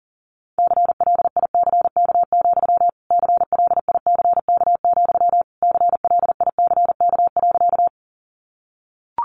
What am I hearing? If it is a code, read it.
Morse code, heard as a keyed tone, about 40 wpm, 700 Hz. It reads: CLICK, CLICK, CLICK.
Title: Sentences from Top 1100 Words 40wpm